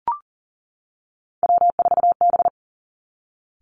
Morse code, heard as a keyed tone, about 38 wpm, 700 Hz. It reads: W4B
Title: Three Challenging Letter-Number Combinations 40wpm